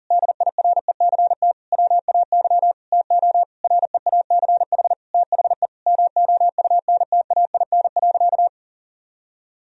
Morse code, heard as a keyed tone, about 37 wpm, 700 Hz. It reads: DIRECT WAY TO REACH THE MOUNTAIN.
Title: Sentences from Top 1000 Words - Review Part 2 - 40wpm